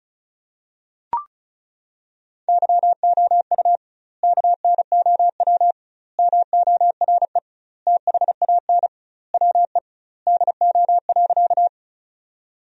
Morse code, heard as a keyed tone, about 36 wpm, 700 Hz. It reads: YOU KNOW MORE THAN WE DO.
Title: 2x Word-Spacing - Sentences from Top 100 Words - 35wpm